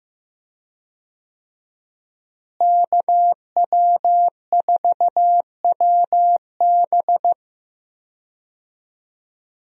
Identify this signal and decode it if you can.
Morse code, heard as a keyed tone, about 15 wpm, 700 Hz. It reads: KW4WB
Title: USA Call Signs - 5 Characters - Repeated 2x - 15wpm